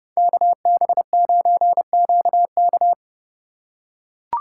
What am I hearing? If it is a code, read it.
Morse code, heard as a keyed tone, about 30 wpm, 700 Hz. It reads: KB9QK